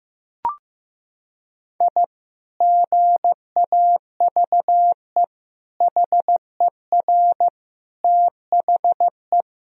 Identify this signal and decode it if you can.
Morse code, heard as a keyed tone, about 15 wpm, 700 Hz. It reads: I GAVE HER THE
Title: Sentences from Top 1000 Words - Review Part 3 - 15wpm